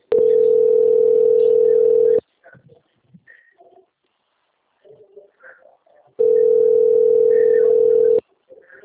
Ringback tone (North America), 440 + 480 Hz.